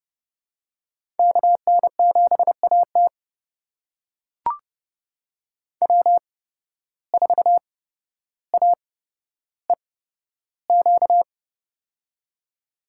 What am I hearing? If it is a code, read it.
Morse code, heard as a keyed tone, about 30 wpm, spaced as for 13 wpm, 700 Hz. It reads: KN7AT W4AEQ
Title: USA Call Signs - 5 Characters - 8x Character Spacing - 30wpm